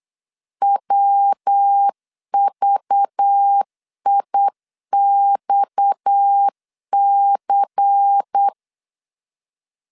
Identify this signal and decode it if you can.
Morse code, heard as a keyed tone, about 8 wpm, 800 Hz. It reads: WVIXC
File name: 8wpm